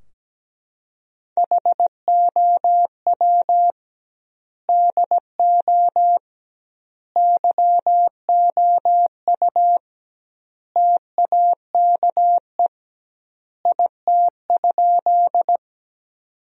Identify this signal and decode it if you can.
Morse code, heard as a keyed tone, about 17 wpm, 700 Hz. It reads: HOW DO YOU TAKE IT?
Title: 2x Word-Spacing - Sentences from Top 200 Words - 17wpm